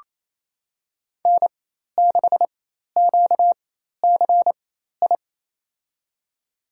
Morse code, heard as a keyed tone, about 27 wpm, spaced as for 13 wpm, 700 Hz. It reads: N6QCI
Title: USA Call Signs - 5 Characters - 4x Character Spacing - 28wpm